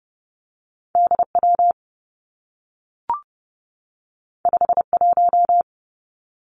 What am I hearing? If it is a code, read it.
Morse code, heard as a keyed tone, about 30 wpm, 700 Hz. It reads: D W 5 1
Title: Two Challenging Letter-Number Combinations 30wpm